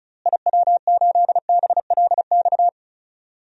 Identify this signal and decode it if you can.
Morse code, heard as a keyed tone, about 36 wpm, 700 Hz. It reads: IW8BLX
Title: DXCC-100 Call Signs - 6 Characters - Repeated 3x - 35wpm